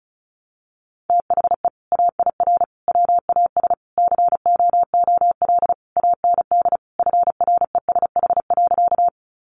Morse code, heard as a keyed tone, about 34 wpm, 700 Hz. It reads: THE AIR WAS COOL AND FRESH.
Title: Sentences from Top 1200 Words 35wpm